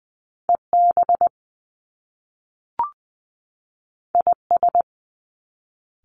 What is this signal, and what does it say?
Morse code, heard as a keyed tone, about 20 wpm, 700 Hz. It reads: EB IS
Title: Top Two Letter Combinations in Words 20wpm